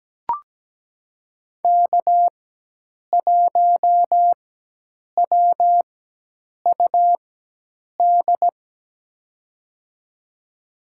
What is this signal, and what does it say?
Morse code, heard as a keyed tone, about 17 wpm, spaced as for 8 wpm, 700 Hz. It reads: K1WUD
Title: USA Call Signs - 5 Characters - 4x Character Spacing - 17wpm